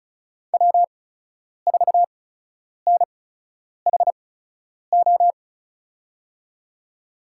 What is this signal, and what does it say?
Morse code, heard as a keyed tone, about 36 wpm, spaced as for 10 wpm, 700 Hz. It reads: W4NHO
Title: USA Call Signs - 5 Characters - 8x Character Spacing - 35wpm